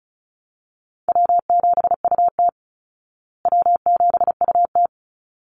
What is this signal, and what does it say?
Morse code, heard as a keyed tone, about 34 wpm, 700 Hz. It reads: W7UT W7UT
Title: Easy Call Signs Repeated 2x - 35wpm